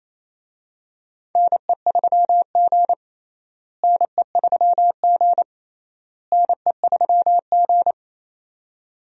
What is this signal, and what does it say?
Morse code, heard as a keyed tone, about 28 wpm, 700 Hz. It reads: NE3G NE3G NE3G
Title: USA Call Signs - 4 Characters - Repeated 3x - 28wpm